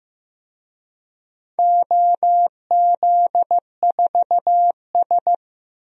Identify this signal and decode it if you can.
Morse code, heard as a keyed tone, about 16 wpm, 700 Hz. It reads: OZ4S
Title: DXCC-100 Call Signs - 4 Characters - Repeated 3x - 15wpm